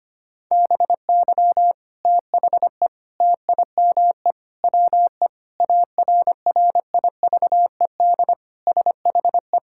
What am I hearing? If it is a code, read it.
Morse code, heard as a keyed tone, about 25 wpm, 700 Hz. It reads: BY THE TIME WE ARRIVED SHE
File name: practice / 25wpm / Sentences with 3-Word Phrases - 25wpm